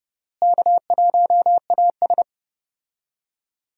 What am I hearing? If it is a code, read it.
Morse code, heard as a keyed tone, about 31 wpm, 700 Hz. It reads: K1AS
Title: USA Call Signs - 4 Characters - Repeated 3x - 30wpm